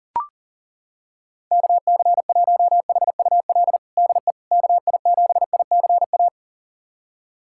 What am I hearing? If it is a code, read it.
Morse code, heard as a keyed tone, about 40 wpm, 700 Hz. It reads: KC1HUL DE KI7ICA
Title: USA Callsign DE Callsign - 6 Characters - 40wpm